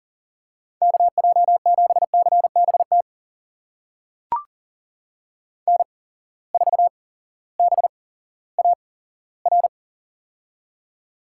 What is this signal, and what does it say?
Morse code, heard as a keyed tone, about 40 wpm, spaced as for 18 wpm, 700 Hz. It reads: KJ7CBT N4BAR
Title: USA Call Signs - 8x Character Spacing - 40wpm